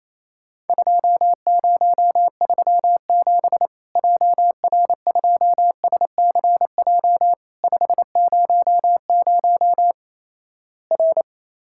Morse code, heard as a keyed tone, about 28 wpm, 700 Hz. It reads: 2037 JR2SCJ 500 R
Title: Worked All Europe DX Contest - QTC Exchange - 28wpm